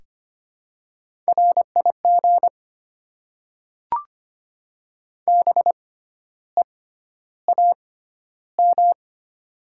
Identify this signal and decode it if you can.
Morse code, heard as a keyed tone, about 25 wpm, spaced as for 12 wpm, 700 Hz. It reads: RIG BEAM
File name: Top 90 QSO Elements - 6x Character Spacing - 25wpm